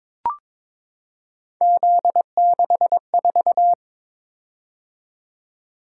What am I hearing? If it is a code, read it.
Morse code, heard as a keyed tone, about 22 wpm, 700 Hz. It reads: Z64
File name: practice / 22wpm / Three Letter-Number Combinations - 22wpm